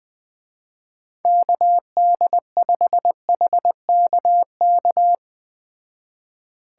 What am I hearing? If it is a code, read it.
Morse code, heard as a keyed tone, about 21 wpm, 700 Hz. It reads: KD5HKK
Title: Speed Racing - USA 2x3 Call Signs 17wpm